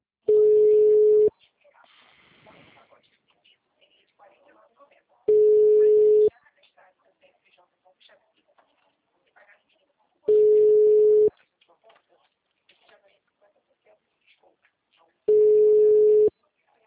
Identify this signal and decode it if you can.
Ringback tone (Europe, 425 Hz), with speech.